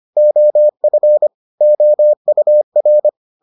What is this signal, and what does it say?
Morse code, heard as a keyed tone, about 25 wpm, 600 Hz. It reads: OF OUR